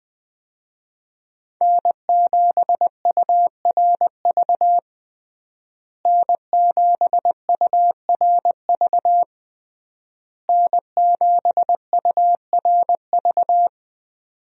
Morse code, heard as a keyed tone, about 20 wpm, 700 Hz. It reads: N7URV N7URV N7URV
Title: USA Call Signs - 5 Characters - Repeated 3x - 20wpm